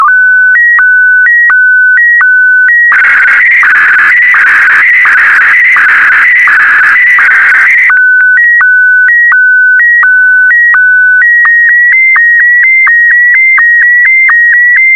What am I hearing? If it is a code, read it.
SSTV (slow-scan television), PD 90.